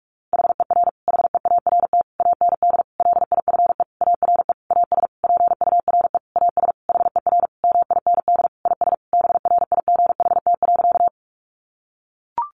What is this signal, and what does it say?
Morse code, heard as a keyed tone, about 44 wpm, 700 Hz. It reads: HER HEART AND LIFE ARE AS PURE AS HER MIND IS BRIGHT.
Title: Sentences from Top 1500 Words 45wpm